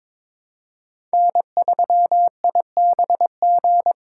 Morse code, heard as a keyed tone, about 22 wpm, 700 Hz. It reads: N3IBG
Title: USA Call Signs - 5 Characters - 2x Character Spacing - 22wpm